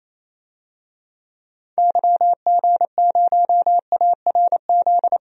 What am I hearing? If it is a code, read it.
Morse code, heard as a keyed tone, about 27 wpm, 700 Hz. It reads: YG0ARZ
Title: DXCC-100 Call Signs - 6 Characters - 28wpm